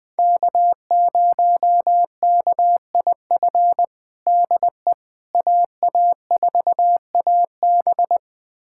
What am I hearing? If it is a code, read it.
Morse code, heard as a keyed tone, about 20 wpm, 700 Hz. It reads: K0KIF DE AA4AB
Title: USA Callsign DE Callsign - 5 Characters - 20wpm